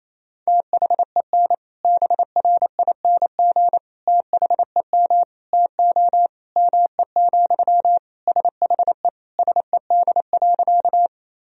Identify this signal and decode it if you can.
Morse code, heard as a keyed tone, about 28 wpm, 700 Hz. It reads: THEN BRING THEM TO ME, SHE SED.
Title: Sentences from Top 500 Words and Abbreviations 28wpm